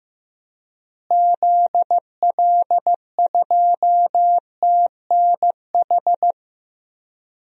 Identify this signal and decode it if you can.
Morse code, heard as a keyed tone, about 15 wpm, 700 Hz. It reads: ZL2TNH
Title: DXCC-100 Call Signs - 6 Characters - Repeated 2x - 15wpm